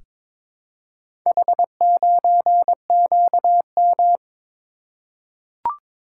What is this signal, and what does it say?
Morse code, heard as a keyed tone, about 22 wpm, 700 Hz. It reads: H9QM
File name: practice / 22wpm / Four Letter-Number Combinations - 22wpm